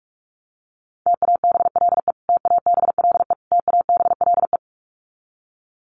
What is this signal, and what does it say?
Morse code, heard as a keyed tone, about 44 wpm, 700 Hz. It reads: TABLE TABLE TABLE
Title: Top 500 Words Repeated 45wpm